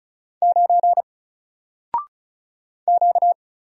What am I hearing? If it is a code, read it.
Morse code, heard as a keyed tone, about 36 wpm, 700 Hz. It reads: 9 Q
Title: Single Letter-Number 35wpm